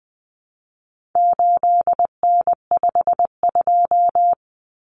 Morse code, heard as a keyed tone, about 20 wpm, 700 Hz. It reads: 8N52
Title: Common Cut Numbers - 4-Digit 20wpm